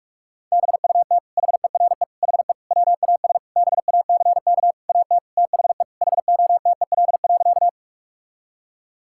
Morse code, heard as a keyed tone, about 46 wpm, 700 Hz. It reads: BUT HERE HE WAS BACK AT THE HOTEL.
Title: Sentences from Top 1600 Words 45wpm